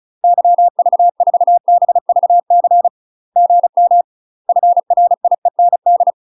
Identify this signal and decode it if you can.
Morse code, heard as a keyed tone, about 36 wpm, 700 Hz. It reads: YV4BVC GM FRIEND
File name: UK MASTER QSOs TRAINING / 35WPM / UK MASTER QSOs 0014